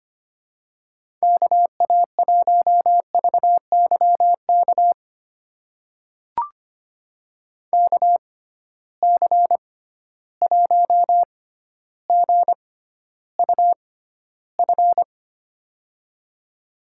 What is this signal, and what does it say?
Morse code, heard as a keyed tone, about 25 wpm, spaced as for 13 wpm, 700 Hz. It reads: KA1VYK KC1GUF
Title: USA Call Signs - 6 Characters - 6x Character Spacing - 25wpm